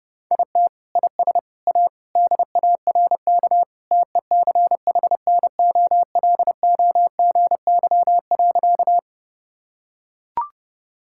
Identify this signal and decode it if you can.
Morse code, heard as a keyed tone, about 32 wpm, 700 Hz. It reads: IT IS A DARK TECHNOLOGY.
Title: Sentences from Top 1000 Words - Review Part 5 - 30wpm